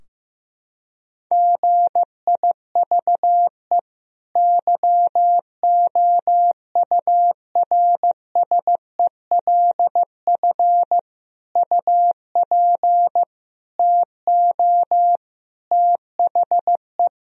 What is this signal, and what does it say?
Morse code, heard as a keyed tone, about 15 wpm, 700 Hz. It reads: GIVE YOURSELF UP TO THE
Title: Sentences from Top 1000 Words - Review Part 4 - 15wpm